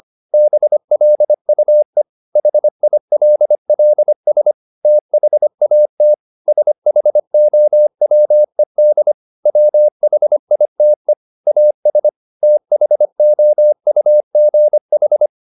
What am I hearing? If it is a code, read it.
Morse code, heard as a keyed tone, about 25 wpm, 600 Hz. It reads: BLUE HILLS THAT SHOWED WHITE AS THOUGH
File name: the old man and the sea part 1 25wpm